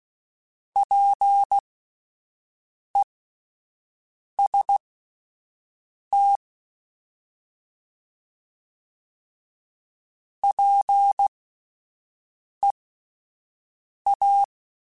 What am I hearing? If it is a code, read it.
Morse code, heard as a keyed tone, about 16 wpm, spaced as for 5 wpm, 800 Hz. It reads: PEST PEA